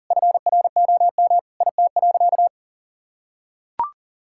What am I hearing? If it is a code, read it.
Morse code, heard as a keyed tone, about 37 wpm, 700 Hz. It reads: FROM IT.
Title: Sentences from Top 1000 Words - Review Part 5 - 40wpm